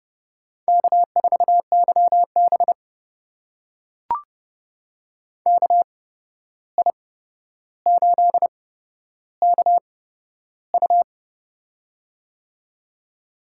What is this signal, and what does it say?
Morse code, heard as a keyed tone, about 30 wpm, spaced as for 12 wpm, 700 Hz. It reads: K4YB KI8KU
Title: USA Call Signs - 8x Character Spacing - 30wpm